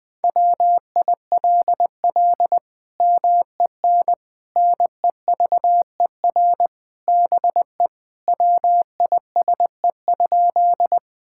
Morse code, heard as a keyed tone, about 21 wpm, 700 Hz. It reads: WILL MEN NEVER BE WISE?
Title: Sentences from Top 1600 Words 20wpm